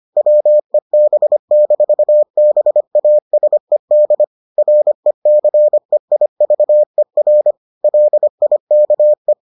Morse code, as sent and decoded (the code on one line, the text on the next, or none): .-- . -... -....- -... .- ... -.. / .-. -.-. .. ...- . .-. / .-.. .. -.- .
WEB-BASED RECEIVER LIKE